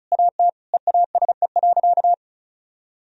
Morse code, heard as a keyed tone, about 36 wpm, 700 Hz. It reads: AT EASE.
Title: Sentences from Top 1000 Words - Review Part 1 - 35wpm